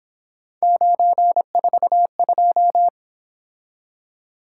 Morse code, heard as a keyed tone, about 27 wpm, 700 Hz. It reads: 942
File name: Speed-Racing - Three Letter-Number Combinations - 22wpm